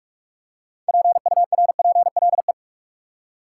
Morse code, heard as a keyed tone, about 43 wpm, 700 Hz. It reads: PURPLE
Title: Six Letter Words - 45wpm